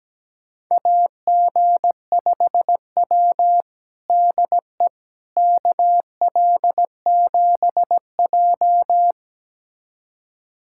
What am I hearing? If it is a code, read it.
Morse code, heard as a keyed tone, about 17 wpm, 700 Hz. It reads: AG5W DE KL7J
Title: USA Callsign DE Callsign - 4 Characters - 17wpm